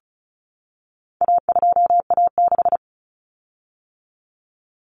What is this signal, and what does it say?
Morse code, heard as a keyed tone, about 34 wpm, 700 Hz. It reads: A2A6
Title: Common Cut Numbers - 4-Digit 35wpm